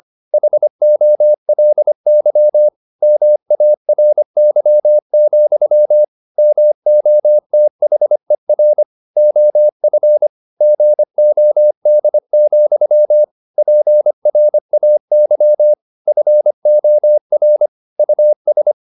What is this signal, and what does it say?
Morse code, heard as a keyed tone, about 25 wpm, 600 Hz. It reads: HOLY MARY, MOTHER OF GOD, PRAY FOR US
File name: the old man and the sea part 1 25wpm